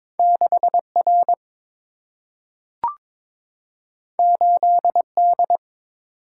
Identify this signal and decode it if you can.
Morse code, heard as a keyed tone, about 22 wpm, 700 Hz. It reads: 6 R 8 D
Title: Single-Letter Single-Number 22wpm